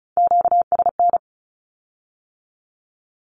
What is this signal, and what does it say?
Morse code, heard as a keyed tone, about 36 wpm, 700 Hz. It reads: QSN
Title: Q-Codes with Spoken Meaning - 35wpm